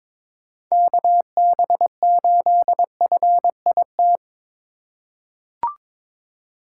Morse code, heard as a keyed tone, about 22 wpm, 700 Hz. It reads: KB8FIT
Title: USA Call Signs - 6 Characters - 2x Character Spacing - 22wpm